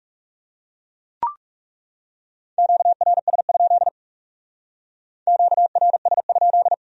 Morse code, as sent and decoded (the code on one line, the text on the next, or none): --.- .-. ... ..--.. / --.- .-. ... ..--..
QRS? QRS?